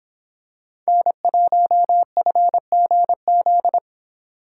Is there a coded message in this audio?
Yes: Morse code from 0.9 to 3.8 s, heard as a keyed tone, about 25 wpm, 700 Hz. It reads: N1FGZ